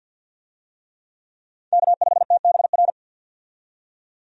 Morse code, heard as a keyed tone, about 49 wpm, 700 Hz. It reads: K5TBR
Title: USA Call Signs - 5 Characters - 50wpm